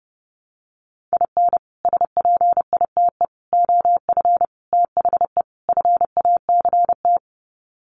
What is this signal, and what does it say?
Morse code, heard as a keyed tone, about 28 wpm, 700 Hz. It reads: IN SPITE OF THE FACT
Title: Sets of 5 Words Encore2 30wpm